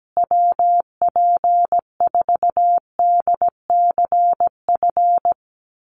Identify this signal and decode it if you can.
Morse code, heard as a keyed tone, about 17 wpm, 700 Hz. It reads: WP4DCF
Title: USA Call Signs Repeated 2x 17wpm